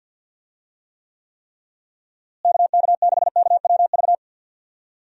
Morse code, heard as a keyed tone, about 51 wpm, 700 Hz. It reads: KK6KWV
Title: USA Call Signs - 6 Characters - Repeated 3x - 50wpm